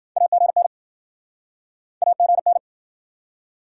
Morse code, heard as a keyed tone, about 60 wpm, 700 Hz. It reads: AGN AGN